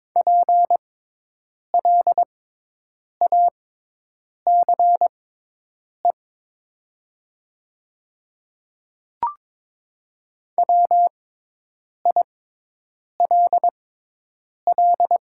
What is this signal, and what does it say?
Morse code, heard as a keyed tone, about 22 wpm, spaced as for 8 wpm, 700 Hz. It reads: PLACE WILL
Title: 6x Character Spacing - Top 200 Words - 22wpm